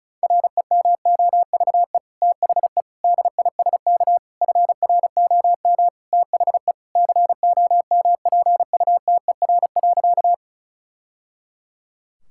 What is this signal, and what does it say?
Morse code, heard as a keyed tone, about 36 wpm, 700 Hz. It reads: REMOVE THE DISK FROM THE COMPUTER.
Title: Sentences from Top 1000 Words - Review Part 6 - 35wpm